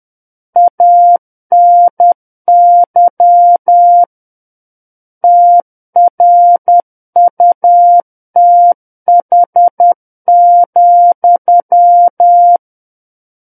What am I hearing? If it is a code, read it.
Morse code, heard as a keyed tone, about 10 wpm, 700 Hz. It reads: ANY TRUTH,
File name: Book One / 01-Chapter 1 - 10 WPM